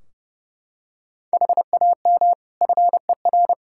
Morse code, heard as a keyed tone, about 30 wpm, 700 Hz. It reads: HAM FER
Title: Practice QSO - Exchange 3 - More Info - 30wpm